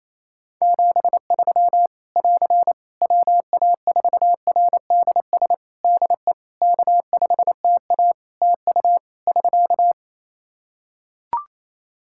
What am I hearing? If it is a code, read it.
Morse code, heard as a keyed tone, about 27 wpm, 700 Hz. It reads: 73 <AR> WA4RDS DE K5TA TU <SK>
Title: Practice QSO - Exchange 4 - Ending - 28wpm